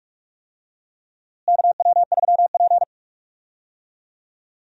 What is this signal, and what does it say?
Morse code, heard as a keyed tone, about 46 wpm, 700 Hz. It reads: KW3P
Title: USA Call Signs - 4 Characters - Repeated 3x - 45wpm